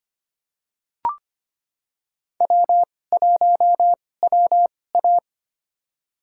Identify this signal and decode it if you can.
Morse code, heard as a keyed tone, about 25 wpm, spaced as for 18 wpm, 700 Hz. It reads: W1WA
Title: USA Call Signs - 4 Characters - 2x Character Spacing - 25wpm